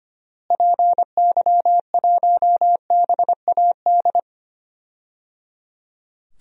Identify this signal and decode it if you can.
Morse code, heard as a keyed tone, about 25 wpm, 700 Hz. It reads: PY1BAD
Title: DXCC-100 Call Signs - 6 Characters - Repeated 3x - 25wpm